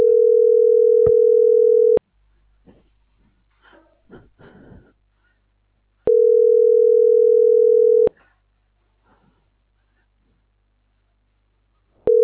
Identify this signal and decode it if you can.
Ringback tone (North America), 440 + 480 Hz.